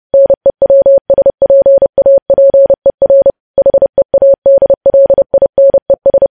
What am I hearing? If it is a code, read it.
Morse code, heard as a keyed tone, about 30 wpm, 550 Hz. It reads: NEWSPAPER HEADLINES
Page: news headlines in morse code at 30 wpm